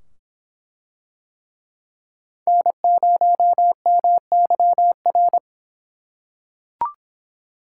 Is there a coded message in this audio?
Yes: Morse code from 2.5 to 5.4 s, heard as a keyed tone, about 24 wpm, 700 Hz. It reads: N0MYR